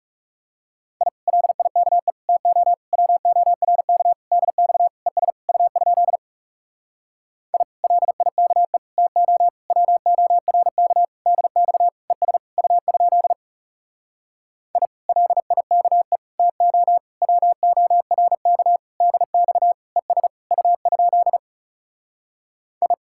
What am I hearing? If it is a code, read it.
Morse code, heard as a keyed tone, about 37 wpm, 700 Hz. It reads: I LIKE TO WORK DX ES U? I LIKE TO WORK DX ES U? I LIKE TO WORK DX ES U? I
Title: Speed-Racing Common QSO Phrases - 30wpm